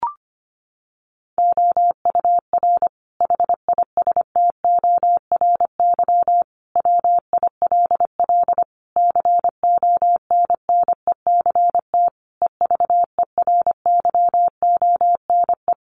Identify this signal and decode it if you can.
Morse code, heard as a keyed tone, about 25 wpm, 700 Hz. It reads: OUR HISTORY WILL CONNECT EVERYONE
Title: Sentences from Top 1500 Words 25wpm